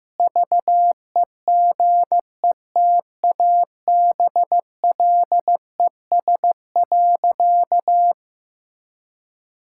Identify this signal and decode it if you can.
Morse code, heard as a keyed tone, about 16 wpm, 700 Hz. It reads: VEGETABLES.
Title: Pangram Challenge v2 - 15wpm